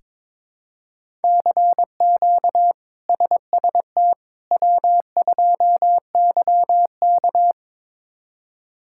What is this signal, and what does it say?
Morse code, heard as a keyed tone, about 22 wpm, 700 Hz. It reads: CQ SST W2YK
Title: K1USN SST QSO Practice - 22wpm